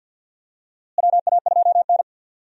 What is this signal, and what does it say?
Morse code, heard as a keyed tone, about 46 wpm, 700 Hz. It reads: WA2N
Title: USA Call Signs - 4 Characters - Repeated 2x - 50wpm